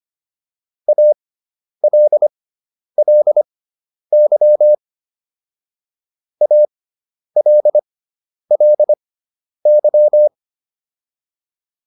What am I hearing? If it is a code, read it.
Morse code, heard as a keyed tone, about 25 wpm, spaced as for 10 wpm, 600 Hz. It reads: ALLY ALLY